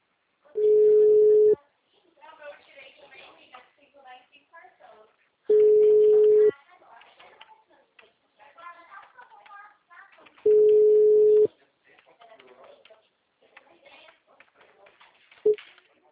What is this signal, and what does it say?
Ringback tone (Europe, 425 Hz), with speech.
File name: out- -2013- - - .6780